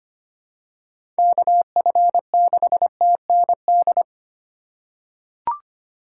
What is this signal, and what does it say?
Morse code, heard as a keyed tone, about 25 wpm, 700 Hz. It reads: KF6TND
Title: USA Call Signs - 6 Characters - 4x Character Spacing - 25wpm